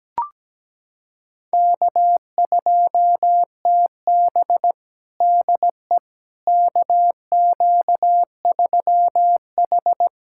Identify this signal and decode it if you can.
Morse code, heard as a keyed tone, about 17 wpm, 700 Hz. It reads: K2TB DE KQ3H
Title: USA Callsign DE Callsign - 4 Characters - 17wpm